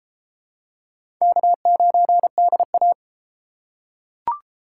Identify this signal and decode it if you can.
Morse code, heard as a keyed tone, about 30 wpm, 700 Hz. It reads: K9DA